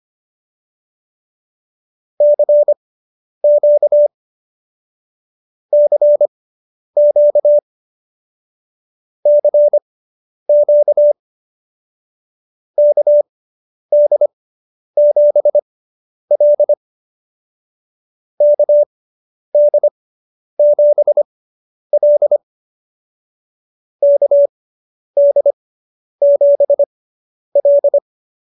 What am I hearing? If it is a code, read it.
Morse code, heard as a keyed tone, about 25 wpm, spaced as for 10 wpm, 590 Hz. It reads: CQ CQ CQ KD7L KD7L KD7L